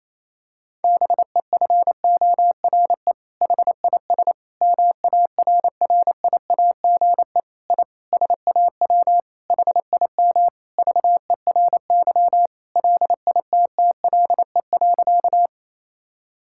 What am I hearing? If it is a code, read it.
Morse code, heard as a keyed tone, about 28 wpm, 700 Hz. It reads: BEFORE HIS MARRIAGE I SAW HIM VERY LITTLE.
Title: Sentences from Top 1000 Words - Review Part 5 - 28wpm